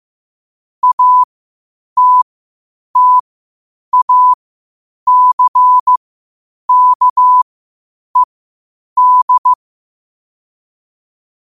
Morse code, heard as a keyed tone, about 15 wpm, spaced as for 8 wpm, 1000 Hz. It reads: ATTACKED
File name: qotdfiles / 8wpm